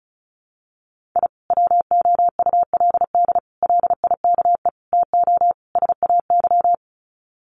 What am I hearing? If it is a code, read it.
Morse code, heard as a keyed tone, about 34 wpm, 700 Hz. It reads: I WOULD LIKE TO SAY
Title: Sets of 5 Words Encore2 35wpm